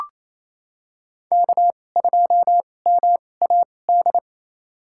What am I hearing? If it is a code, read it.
Morse code, heard as a keyed tone, about 28 wpm, spaced as for 20 wpm, 700 Hz. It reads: K2MAD
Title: USA Call Signs - 2x Character Spacing - 28wpm